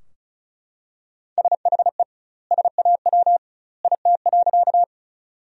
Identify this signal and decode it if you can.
Morse code, heard as a keyed tone, about 36 wpm, 700 Hz. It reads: SHE SAW IT.
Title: 2x Word-Spacing - Sentences from Top 300 Words - 35wpm